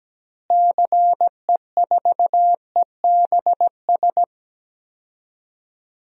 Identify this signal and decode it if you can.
Morse code, heard as a keyed tone, about 17 wpm, 700 Hz. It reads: CE4EBS
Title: DXCC-100 Call Signs - 6 Characters - Repeated 2x - 17wpm